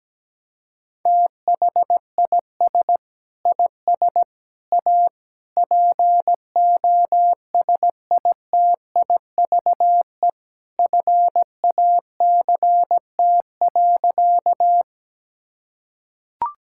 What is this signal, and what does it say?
Morse code, heard as a keyed tone, about 17 wpm, 700 Hz. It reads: THIS IS A POSITIVE FACT.
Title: Sentences from Top 1000 Words - Review Part 5 - 17wpm